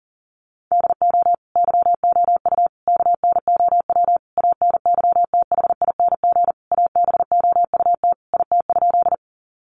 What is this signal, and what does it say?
Morse code, heard as a keyed tone, about 40 wpm, 700 Hz. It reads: DO YOU KNOW ANYTHING ABOUT IT?
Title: Sentences from Top 1000 Words 40wpm